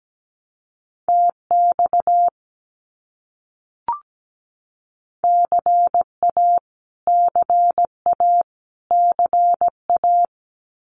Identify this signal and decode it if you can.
Morse code, heard as a keyed tone, about 17 wpm, 700 Hz. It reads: TX CA CA CA
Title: US States Repeated 3x 17wpm